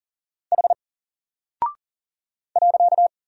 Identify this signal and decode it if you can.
Morse code, heard as a keyed tone, about 40 wpm, 700 Hz. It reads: H .
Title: Single Character (Letter:Number:Punctuation:Prosign) Alt 40wpm